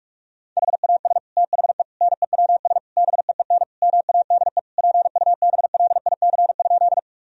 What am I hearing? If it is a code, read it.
Morse code, heard as a keyed tone, about 41 wpm, 700 Hz. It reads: HAS THE NEWS BEEN MADE PUBLIC?
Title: Sentences from Top 1000 Words - Review Part 3 - 45wpm